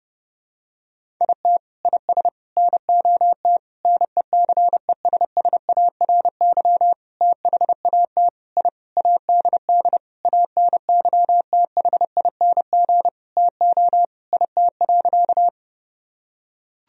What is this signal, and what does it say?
Morse code, heard as a keyed tone, about 30 wpm, 700 Hz. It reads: IT IS NOT NECESSARY THAT I ADD ANYTHING TO IT.
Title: Sentences from Top 1000 Words - Review Part 1 - 30wpm